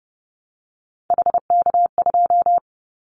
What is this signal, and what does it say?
Morse code, heard as a keyed tone, about 30 wpm, 700 Hz. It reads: HK2
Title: Callsign Prefixes for Top 100 DXCC - 30wpm